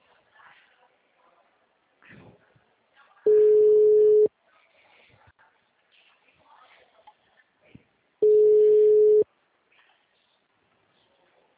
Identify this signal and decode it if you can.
Ringback tone (Europe, 425 Hz), with speech.